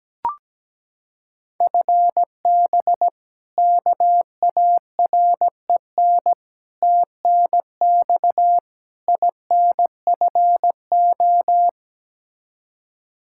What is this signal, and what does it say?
Morse code, heard as a keyed tone, about 17 wpm, 700 Hz. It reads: FB KAREN TNX INFO
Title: K1USN SST QSO Practice 17wpm